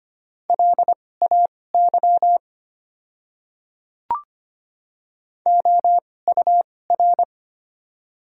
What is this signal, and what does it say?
Morse code, heard as a keyed tone, about 25 wpm, spaced as for 18 wpm, 700 Hz. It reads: LAY OUR
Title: 2x Character Spacing - Top 500 Words - 25wpm